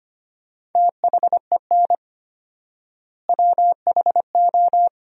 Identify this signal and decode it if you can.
Morse code, heard as a keyed tone, about 25 wpm, 700 Hz. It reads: THEN WHO
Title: 4x Word-Spacing - Sentences from Top 100 Words - 25wpm